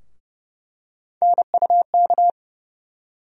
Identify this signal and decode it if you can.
Morse code, heard as a keyed tone, about 32 wpm, 700 Hz. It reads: NUK
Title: USA Call Sign Postfixes - 3 Letters 30wpm